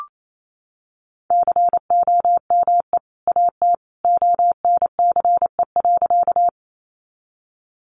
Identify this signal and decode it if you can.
Morse code, heard as a keyed tone, about 28 wpm, 700 Hz. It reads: COME AT ONCE.
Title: Sentences from Top 300 Words 28wpm